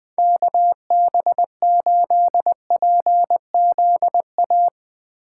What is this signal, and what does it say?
Morse code, heard as a keyed tone, about 20 wpm, 700 Hz. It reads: KB8PZA
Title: USA Call Signs - 6 Characters - 2x Character Spacing - 20wpm